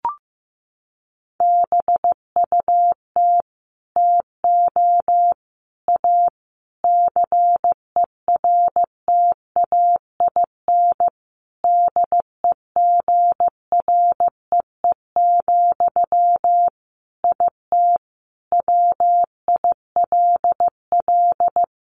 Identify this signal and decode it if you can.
Morse code, heard as a keyed tone, about 15 wpm, 700 Hz. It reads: BUT TO A CERTAIN DEGREE, IT WILL
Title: Sentences from Top 1500 Words 15wpm